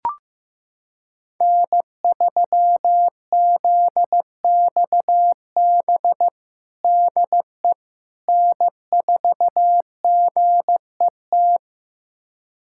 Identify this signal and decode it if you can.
Morse code, heard as a keyed tone, about 15 wpm, 700 Hz. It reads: N3ZXB DE N4GET
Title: USA Callsign DE Callsign - 5 Characters - 15wpm